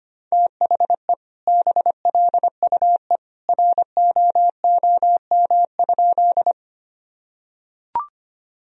Morse code, heard as a keyed tone, about 25 wpm, 700 Hz. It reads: THE BLUE ROOM?
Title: Sentences from Top 1000 Words - Review Part 3 - 25wpm